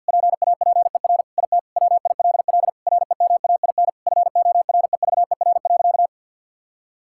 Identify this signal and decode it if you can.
Morse code, heard as a keyed tone, about 50 wpm, 700 Hz. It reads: PAPER IT WILL REMAIN FOREVER.
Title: Sentences from Top 1600 Words 50wpm